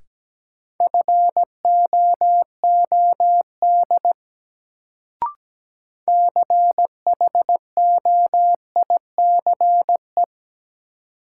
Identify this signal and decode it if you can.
Morse code, heard as a keyed tone, about 17 wpm, 700 Hz. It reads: FOOD CHOICE
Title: Sets of 5 Words - New Word Drill - 17wpm